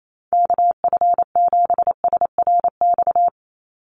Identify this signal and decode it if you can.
Morse code, heard as a keyed tone, about 28 wpm, 700 Hz. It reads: KF7SRX
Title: USA Call Signs 28wpm